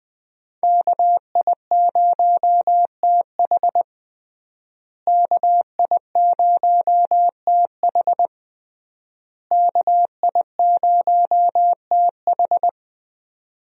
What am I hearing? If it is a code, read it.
Morse code, heard as a keyed tone, about 20 wpm, 700 Hz. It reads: KI0TH KI0TH KI0TH
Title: USA Call Signs - 5 Characters - Repeated 3x - 20wpm